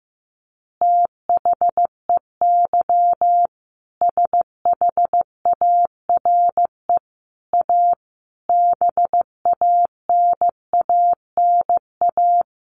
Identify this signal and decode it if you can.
Morse code, heard as a keyed tone, about 15 wpm, 700 Hz. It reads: THEY SHARE A BANANA
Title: Additional Easy Phrases 15wpm